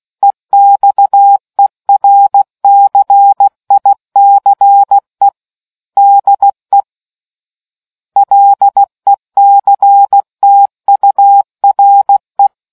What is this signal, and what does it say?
Morse code, heard as a keyed tone, about 16 wpm, 800 Hz. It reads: EXERCICE DE LECTURE